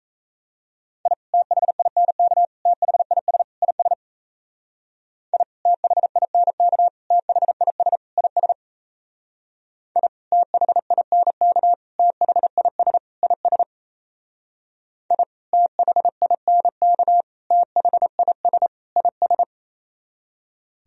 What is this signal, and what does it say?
Morse code, heard as a keyed tone, about 34 wpm, 700 Hz. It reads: I THINK THIS IS I THINK THIS IS I THINK THIS IS I THINK THIS IS
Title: Speed-Racing - Sets of 4 Words 28wpm